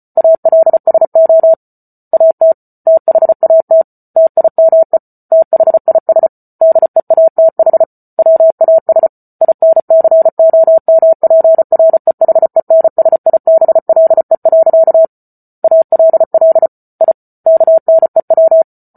Morse code, heard as a keyed tone, about 34 wpm, 650 Hz. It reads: ALSO AT THAT TIME THIS DEATH WAS INCOMPREHENSIBLE. ALL I KNEW